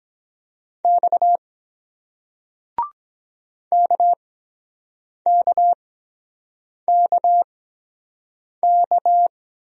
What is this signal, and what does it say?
Morse code, heard as a keyed tone, about 23 wpm, spaced as for 7 wpm, 700 Hz. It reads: X KKKK